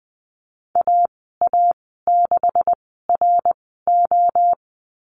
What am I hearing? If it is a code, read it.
Morse code, heard as a keyed tone, about 20 wpm, spaced as for 14 wpm, 700 Hz. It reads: AA6RO
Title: 154 Lesson 25g - Letter p - Call Signs 20wpm